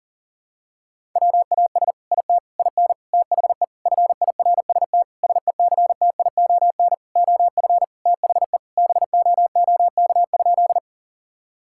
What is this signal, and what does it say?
Morse code, heard as a keyed tone, about 40 wpm, 700 Hz. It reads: WAS IT IN THE FIRST SECTION OF THE BOOK?
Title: Sentences from Top 1000 Words - Review Part 5 - 40wpm